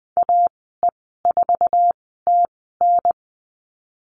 Morse code, heard as a keyed tone, about 20 wpm, spaced as for 14 wpm, 700 Hz. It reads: AE4TN
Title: 069 Lesson 12g - Letter d - Call Signs 20wpm